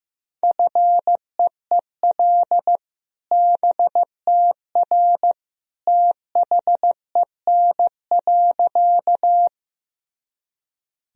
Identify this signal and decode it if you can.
Morse code, heard as a keyed tone, about 15 wpm, 700 Hz. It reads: FEEL BTR THEN.
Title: Sentences from Top 400 Words and Abbreviations 15wpm